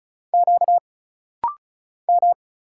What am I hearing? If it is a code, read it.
Morse code, heard as a keyed tone, about 33 wpm, 700 Hz. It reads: Q M
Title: Single Letter-Number Alt - Rapid-Fire 35wpm